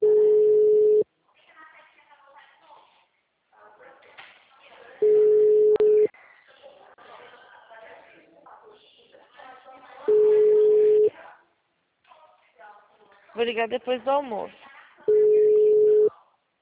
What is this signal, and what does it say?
Ringback tone (Europe, 425 Hz), with speech.
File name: out- -2014- - - .8455